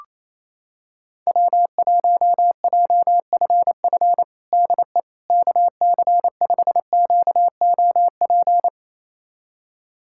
Morse code, heard as a keyed tone, about 28 wpm, 700 Hz. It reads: W1JFF DE KC5QOP
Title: USA Callsign DE Callsign 28wpm